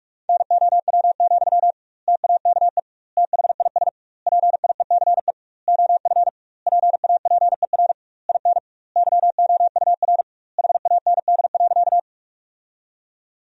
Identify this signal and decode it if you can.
Morse code, heard as a keyed tone, about 46 wpm, 700 Hz. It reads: NOW, TAKE THIS PIECE OF PAPER IN YOUR HAND.
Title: 2x Word-Spacing - Sentences from Top 400 Words - 45wpm